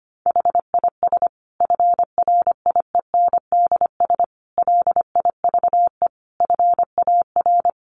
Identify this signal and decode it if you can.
Morse code, heard as a keyed tone, about 25 wpm, 700 Hz. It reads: HIS FRIENDS LIVE FAR
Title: Additional Easy Phrases 25wpm